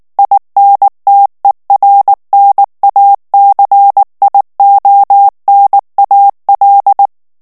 Morse code, heard as a keyed tone, about 19 wpm, 800 Hz. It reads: INTERNACIONAL
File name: morse mp3 / ESPAGNOL TEXTE EN CLAIR 2 VITESSE 18 MOTS MINUTES